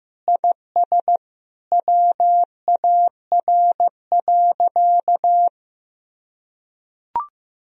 Morse code, heard as a keyed tone, about 16 wpm, 700 Hz. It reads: IS WAR.
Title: Sentences from Top 1000 Words - Review Part 2 - 15wpm